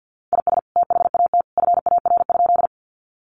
Morse code, heard as a keyed tone, about 50 wpm, 700 Hz. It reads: IS THAT FAR?
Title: Sentences from Top 300 Words 50wpm